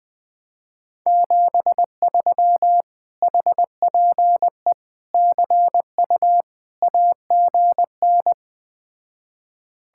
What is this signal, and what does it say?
Morse code, heard as a keyed tone, about 20 wpm, 700 Hz. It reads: 73 HPE CU AGN
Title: Speed-Racing Common QSO Phrases Encore - 17wpm